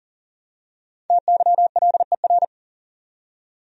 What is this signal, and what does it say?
Morse code, heard as a keyed tone, about 40 wpm, 700 Hz. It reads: TYLER
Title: USA Common Names - Max 5 Letters - 40wpm